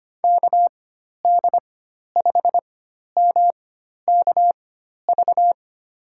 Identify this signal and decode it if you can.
Morse code, heard as a keyed tone, about 25 wpm, spaced as for 12 wpm, 700 Hz. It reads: KD5MKV
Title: USA Call Signs - 6 Characters - 4x Character Spacing - 25wpm